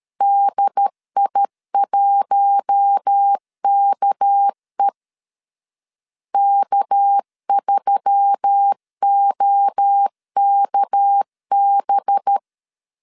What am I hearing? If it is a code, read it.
Morse code, heard as a keyed tone, about 13 wpm, 800 Hz. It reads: DI1KE K3OKB
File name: LEZIONI / 6 PKX0 / 6 PK 12wpm 2006